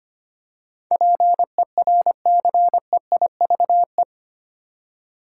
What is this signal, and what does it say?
Morse code, heard as a keyed tone, about 25 wpm, 700 Hz. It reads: PERCEIVE
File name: Eight Letter Words - 25wpm